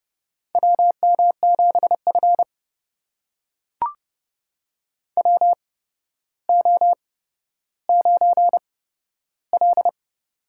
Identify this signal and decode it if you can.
Morse code, heard as a keyed tone, about 31 wpm, spaced as for 13 wpm, 700 Hz. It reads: WM7F WO9L
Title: USA Call Signs - 4 Characters - 8x Character Spacing - 30wpm